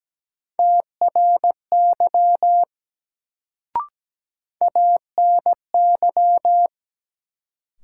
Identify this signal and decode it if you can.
Morse code, heard as a keyed tone, about 17 wpm, 700 Hz. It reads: TRY ANY
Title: Sentences from Top 200 Words - New Word Drill - 17wpm